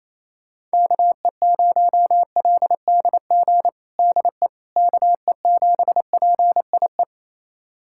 Morse code, heard as a keyed tone, about 28 wpm, 700 Hz. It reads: KE0LDG DE KE7PIE
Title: USA Callsign DE Callsign - 6 Characters - 28wpm